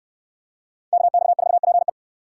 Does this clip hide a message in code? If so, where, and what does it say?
Yes: Morse code from 0.9 to 1.9 s, heard as a keyed tone, about 64 wpm, 700 Hz. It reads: KX4PE